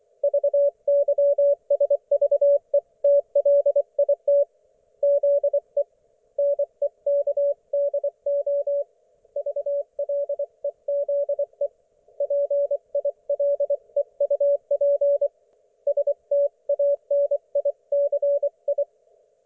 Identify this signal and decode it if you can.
Morse code, heard as a keyed tone, about 24 wpm, 570 Hz. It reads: VYSVETLIT ZE NEKDO VLEZE PILEUP STANICI
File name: Archiv / CUC224 / ok1fhi cul224